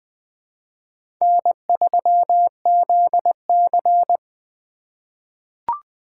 Morse code, heard as a keyed tone, about 20 wpm, 700 Hz. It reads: N3ZC
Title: USA Call Signs - 4 Characters - 3x Character Spacing - 20wpm